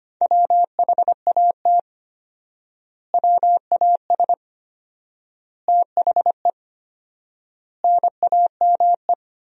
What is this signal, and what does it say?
Morse code, heard as a keyed tone, about 25 wpm, 700 Hz. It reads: WHAT WAS THE NAME
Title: 4x Word-Spacing - Sentences from Top 300 Words - 25wpm